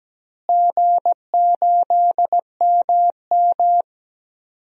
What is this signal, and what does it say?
Morse code, heard as a keyed tone, about 17 wpm, 700 Hz. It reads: G8MM
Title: DXCC-100 Call Signs - 4 Characters - Repeated 3x - 17wpm